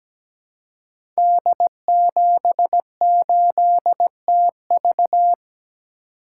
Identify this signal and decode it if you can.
Morse code, heard as a keyed tone, about 17 wpm, 700 Hz. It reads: D78TV
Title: Code Groups - 5 Letter:Numbers - 17wpm